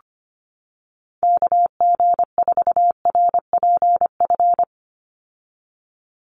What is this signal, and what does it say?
Morse code, heard as a keyed tone, about 25 wpm, 700 Hz. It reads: KG4RPF
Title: USA Call Signs 25wpm